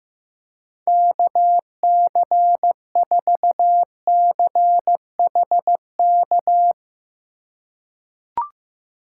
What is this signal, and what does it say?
Morse code, heard as a keyed tone, about 15 wpm, 700 Hz. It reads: KC4CHK
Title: USA Call Signs - 6 Characters - 2x Character Spacing - 15wpm